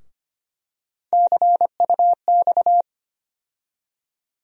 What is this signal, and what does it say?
Morse code, heard as a keyed tone, about 25 wpm, 700 Hz. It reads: CUX
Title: USA Call Sign Postfixes - 3 Letters 25wpm